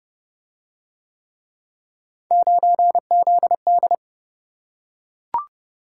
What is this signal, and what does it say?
Morse code, heard as a keyed tone, about 27 wpm, 700 Hz. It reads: 9ZD